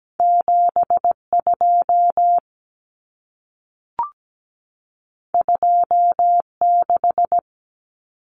Morse code, heard as a keyed tone, about 17 wpm, 700 Hz. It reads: 72 26
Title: Numbers - 2-Digit 17wpm